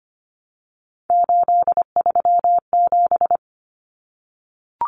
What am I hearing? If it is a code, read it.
Morse code, heard as a keyed tone, about 25 wpm, 700 Hz. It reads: 837
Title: Numbers - 3-Digit 25wpm